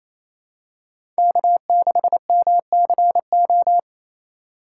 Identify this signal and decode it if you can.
Morse code, heard as a keyed tone, about 28 wpm, 700 Hz. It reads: K6MCO
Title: USA Call Signs - 2x Character Spacing - 28wpm